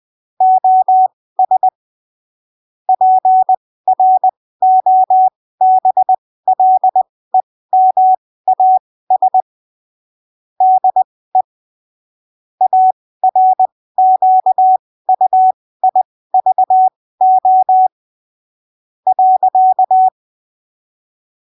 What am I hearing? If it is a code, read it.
Morse code, heard as a keyed tone, about 20 wpm, spaced as for 15 wpm, 750 Hz. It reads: OS PROBLEMAS DE ARQUIVO .
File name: 20ppm